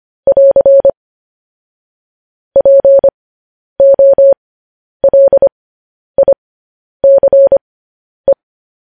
Morse code, heard as a keyed tone, about 25 wpm, spaced as for 10 wpm, 550 Hz. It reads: <AR> POLICE
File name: podcast / News headlines-1x-25x10